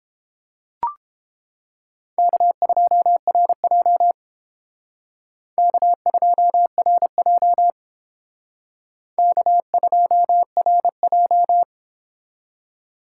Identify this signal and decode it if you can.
Morse code, heard as a keyed tone, about 29 wpm, 700 Hz. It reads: K2RJ K2RJ K2RJ